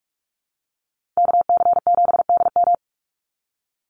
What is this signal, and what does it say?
Morse code, heard as a keyed tone, about 46 wpm, 700 Hz. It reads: KC7DM
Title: NA SOTA Activator Callsigns 45wpm